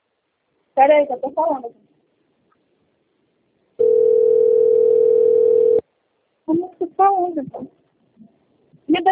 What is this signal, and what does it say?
Ringback tone (North America), 440 + 480 Hz, with speech.
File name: out- - - - - .73720